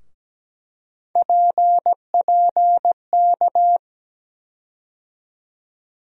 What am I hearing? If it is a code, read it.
Morse code, heard as a keyed tone, about 17 wpm, 700 Hz. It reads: PPK
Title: USA Call Sign Postfixes - 3 Letters 17wpm